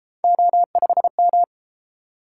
Morse code, heard as a keyed tone, about 32 wpm, 700 Hz. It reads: O5M